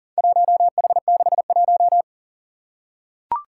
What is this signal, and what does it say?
Morse code, heard as a keyed tone, about 40 wpm, 690 Hz. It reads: 1H61
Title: Four Challenging Letter-Number Combinations 40wpm